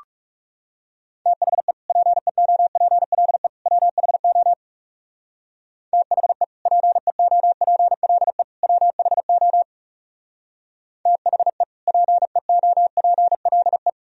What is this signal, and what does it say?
Morse code, heard as a keyed tone, about 39 wpm, 700 Hz. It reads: THE PEOPLE WHO THE PEOPLE WHO THE PEOPLE